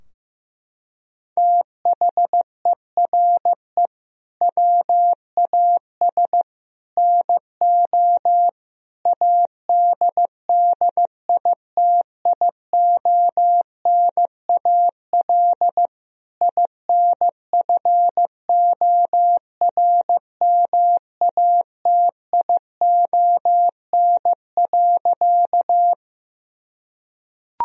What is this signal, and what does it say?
Morse code, heard as a keyed tone, about 15 wpm, 700 Hz. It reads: THERE WAS NO ADDITIONAL INFORMATION.
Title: Sentences from Top 1000 Words - Review Part 4 - 15wpm